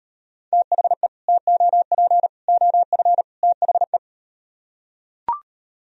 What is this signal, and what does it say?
Morse code, heard as a keyed tone, about 38 wpm, 700 Hz. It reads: THE TOP OF THE